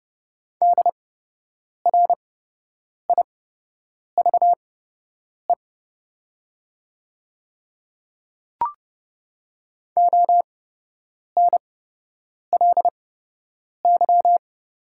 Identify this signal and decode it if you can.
Morse code, heard as a keyed tone, about 32 wpm, spaced as for 8 wpm, 700 Hz. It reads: DRIVE ONLY
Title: 8x Character Spacing - Top 500 Words - 30wpm